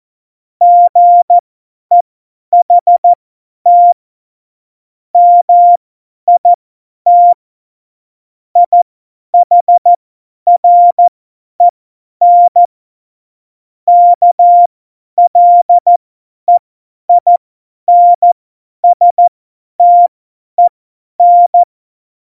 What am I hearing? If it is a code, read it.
Morse code, heard as a keyed tone, about 14 wpm, spaced as for 10 wpm, 700 Hz. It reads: GEHT MIT IHREN KLEINSTEN
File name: Morsekurs / Witze / Witz 10.Juli 2018, 50 F70